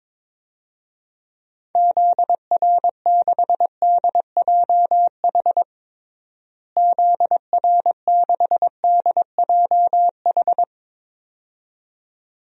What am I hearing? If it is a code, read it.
Morse code, heard as a keyed tone, about 22 wpm, 700 Hz. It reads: ZR6DJH ZR6DJH
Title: DXCC-100 Call Signs - 6 Characters - Repeated 2x - 22wpm